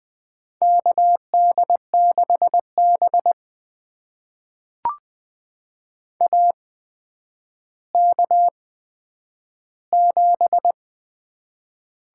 Morse code, heard as a keyed tone, about 20 wpm, spaced as for 10 wpm, 700 Hz. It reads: KD6B AK7
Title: USA Call Signs - 4 Characters - 8x Character Spacing - 20wpm